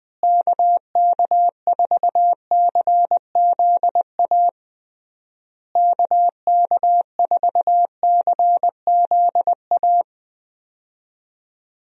Morse code, heard as a keyed tone, about 20 wpm, 700 Hz. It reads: KK4CZA KK4CZA
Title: USA Call Signs - 6 Characters - Repeated 3x - 20wpm